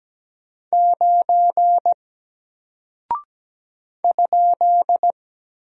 Morse code, heard as a keyed tone, about 17 wpm, 700 Hz. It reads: 9 ?
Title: Single Character (Letter:Number:Punctuation:Prosign) Alt 17wpm